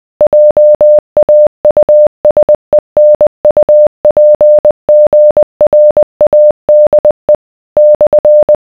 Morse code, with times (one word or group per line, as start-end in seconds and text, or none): 0.2-7.4 JAUSENUPGRADE
7.8-8.6 /